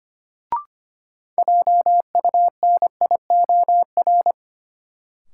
Morse code, heard as a keyed tone, about 25 wpm, 700 Hz. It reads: JUNIOR
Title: Sentences from Top 1000 Words - Review Part 7 - New Word Drill - 25wpm